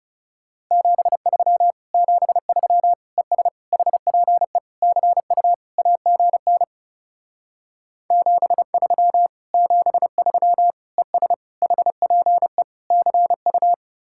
Morse code, heard as a keyed tone, about 37 wpm, 700 Hz. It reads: 73 73 ES HPE CU AGN 73 73 ES HPE CU
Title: Speed-Racing Common QSO Phrases - 30wpm